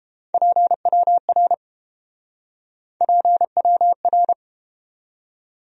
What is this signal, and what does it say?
Morse code, heard as a keyed tone, about 30 wpm, 700 Hz. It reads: PWR PWR